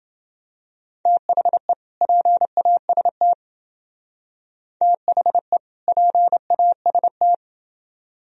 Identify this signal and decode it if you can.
Morse code, heard as a keyed tone, about 28 wpm, 700 Hz. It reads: THE PAST THE PAST